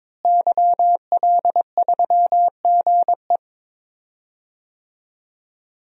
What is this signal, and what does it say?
Morse code, heard as a keyed tone, about 22 wpm, 700 Hz. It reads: YL3GE
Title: DXCC-100 Call Signs - 5 Characters - Repeated 3x - 22wpm